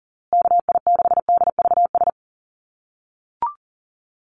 Morse code, heard as a keyed tone, about 40 wpm, 700 Hz. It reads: KI6DVS
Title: USA Call Signs Repeated 2x 40wpm